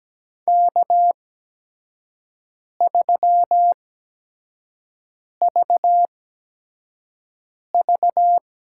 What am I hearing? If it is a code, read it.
Morse code, heard as a keyed tone, about 17 wpm, spaced as for 5 wpm, 700 Hz. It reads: K3VV
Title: USA Call Signs - 5 Characters - 8x Character Spacing - 17wpm